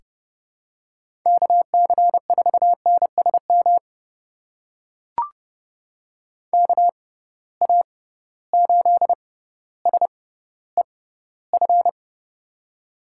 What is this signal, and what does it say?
Morse code, heard as a keyed tone, about 32 wpm, spaced as for 16 wpm, 700 Hz. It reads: KC4NSM KA8SEF
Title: USA Call Signs - 6 Characters - 6x Character Spacing - 30wpm